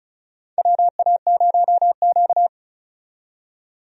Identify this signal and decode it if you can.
Morse code, heard as a keyed tone, about 34 wpm, 700 Hz. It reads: WA0Q
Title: USA Call Signs - 4 Characters - 35wpm